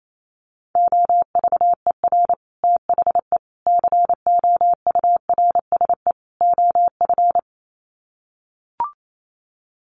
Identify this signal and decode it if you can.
Morse code, heard as a keyed tone, about 28 wpm, 700 Hz. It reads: OVER THE COURSE OF
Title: Sets of 4 Words 28wpm